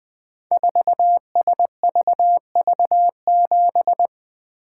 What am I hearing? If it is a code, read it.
Morse code, heard as a keyed tone, about 20 wpm, 700 Hz. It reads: 4SVV7
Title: Code Groups - 5 Letter:Numbers - 20wpm